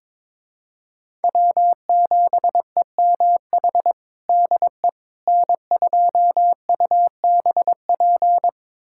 Morse code, heard as a keyed tone, about 22 wpm, 700 Hz. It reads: W7EMH DE N2UBP
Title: USA Callsign DE Callsign - 5 Characters - 22wpm